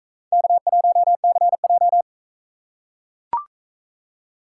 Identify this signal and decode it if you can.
Morse code, heard as a keyed tone, about 39 wpm, 700 Hz. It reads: K1CJ